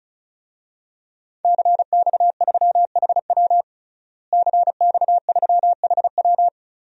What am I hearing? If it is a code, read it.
Morse code, heard as a keyed tone, about 34 wpm, 700 Hz. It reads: CX3HW CX3HW
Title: DXCC-100 Call Signs - 5 Characters - Repeated 3x - 35wpm